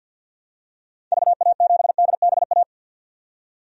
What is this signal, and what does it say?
Morse code, heard as a keyed tone, about 54 wpm, 700 Hz. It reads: VA7DBA
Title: DXCC-100 Call Signs - 6 Characters - 50wpm